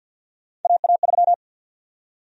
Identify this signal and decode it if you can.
Morse code, heard as a keyed tone, about 46 wpm, 700 Hz. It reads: AA3
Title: USA Call Sign Prefixes - 50wpm